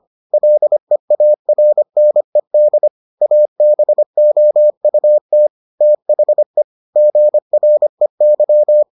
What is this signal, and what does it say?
Morse code, heard as a keyed tone, about 25 wpm, 600 Hz. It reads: LEARNED ABOUT THE GREY